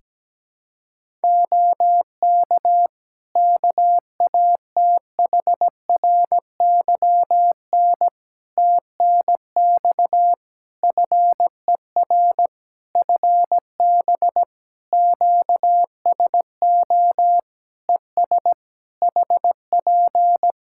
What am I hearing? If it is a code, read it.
Morse code, heard as a keyed tone, about 17 wpm, 700 Hz. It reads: OK KATHRYN TNX FER FB QSO ES HP
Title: K1USN SST QSO Practice 17wpm